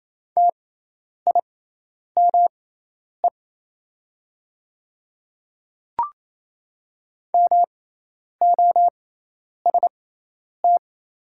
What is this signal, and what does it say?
Morse code, heard as a keyed tone, about 27 wpm, spaced as for 10 wpm, 700 Hz. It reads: TIME MOST
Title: 6x Character Spacing - Top 100 Words - 28wpm